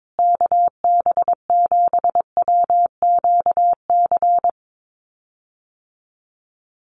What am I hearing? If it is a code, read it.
Morse code, heard as a keyed tone, about 22 wpm, 700 Hz. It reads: KB7WQC
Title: USA Call Signs Repeated 2x 22wpm